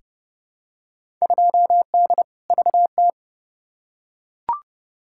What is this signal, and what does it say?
Morse code, heard as a keyed tone, about 30 wpm, 700 Hz. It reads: 2D VT
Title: Field Day Exchanges Encore - 30wpm